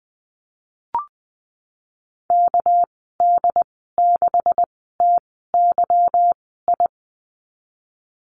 Morse code, heard as a keyed tone, about 20 wpm, spaced as for 15 wpm, 700 Hz. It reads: KD6TYI
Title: 200 Lesson 32g - Letter k - Call Signs 20wpm